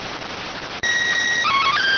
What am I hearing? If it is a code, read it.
SSTV (slow-scan television), PD 290.